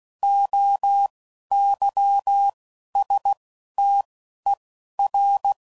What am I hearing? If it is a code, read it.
Morse code, heard as a keyed tone, about 16 wpm, spaced as for 12 wpm, 800 Hz. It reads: OYSTER